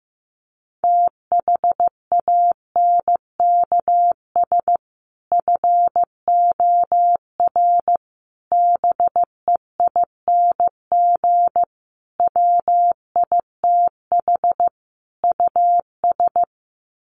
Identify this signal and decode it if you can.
Morse code, heard as a keyed tone, about 15 wpm, 700 Hz. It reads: THANKS FOR BEING WITH US
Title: Sets of 5 Words 15wpm